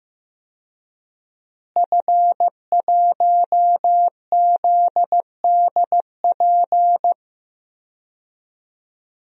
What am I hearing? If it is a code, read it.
Morse code, heard as a keyed tone, about 16 wpm, 700 Hz. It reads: F1ZDP
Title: DXCC-100 Call Signs - 5 Characters - Repeated 3x - 15wpm